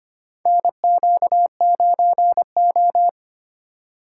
Morse code, heard as a keyed tone, about 25 wpm, 700 Hz. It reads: NQ9O
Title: USA Call Signs - 4 Characters - 2x Character Spacing - 25wpm